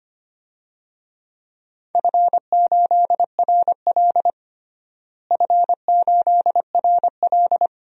Morse code, heard as a keyed tone, about 25 wpm, 700 Hz. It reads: F8RL F8RL
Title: DXCC-100 Call Signs - 4 Characters - Repeated 2x - 25wpm